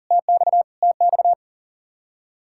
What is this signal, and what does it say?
Morse code, heard as a keyed tone, about 40 wpm, 700 Hz. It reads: TX TX
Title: Practice QSO - Exchange 1 - Beginning - 40wpm